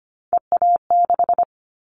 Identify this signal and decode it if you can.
Morse code, heard as a keyed tone, about 25 wpm, 700 Hz. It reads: EA6
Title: Callsign Prefixes for Top 100 DXCC - 25wpm